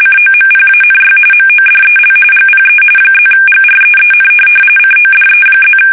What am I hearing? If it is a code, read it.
RTTY, about 45.45 baud. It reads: COMPANY MODEL GPE +GREEN CODE+
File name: fsk850 / teletype-corp